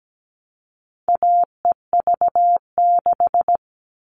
Morse code, heard as a keyed tone, about 17 wpm, 700 Hz. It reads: AEV6
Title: All Cut Numbers - 4-Digit 17wpm